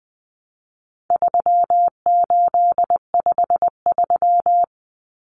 Morse code, heard as a keyed tone, about 20 wpm, 700 Hz. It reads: 3853
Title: Common Cut Numbers - 4-Digit 20wpm